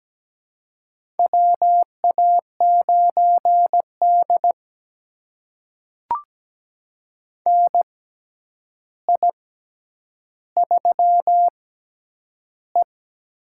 Morse code, heard as a keyed tone, about 17 wpm, spaced as for 9 wpm, 700 Hz. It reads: WA9D NI3E
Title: USA Call Signs - 6x Character Spacing - 17wpm